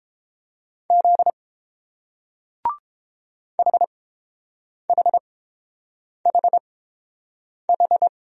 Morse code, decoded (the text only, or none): Z HHHH